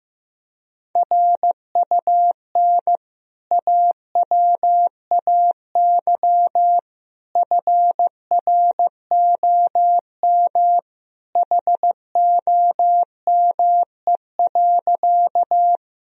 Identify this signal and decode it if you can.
Morse code, heard as a keyed tone, about 15 wpm, 700 Hz. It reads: RUN AWAY FROM HOME.
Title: Sentences from Top 1000 Words - Review Part 1 - 15wpm